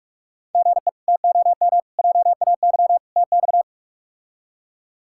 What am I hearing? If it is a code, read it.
Morse code, heard as a keyed tone, about 43 wpm, 700 Hz. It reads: GE TOM JAY TX
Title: K1USN SST QSO Practice - 45wpm